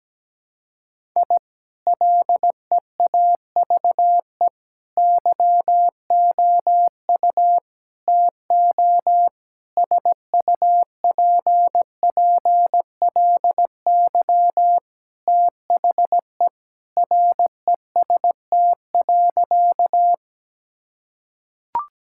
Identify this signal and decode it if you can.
Morse code, heard as a keyed tone, about 17 wpm, 700 Hz. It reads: I LEAVE YOU TO SUPPLY THE REST.
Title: Sentences from Top 1000 Words - Review Part 5 - 17wpm